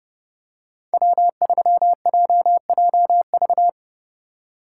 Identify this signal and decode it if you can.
Morse code, heard as a keyed tone, about 30 wpm, 700 Hz. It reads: W3JJV
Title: USA Call Signs - 5 Characters - 2x Character Spacing - 30wpm